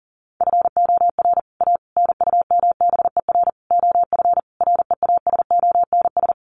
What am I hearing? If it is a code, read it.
Morse code, heard as a keyed tone, about 40 wpm, 700 Hz. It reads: FOR A NUMBER OF REASONS
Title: Sets of 5 Words Encore2 40wpm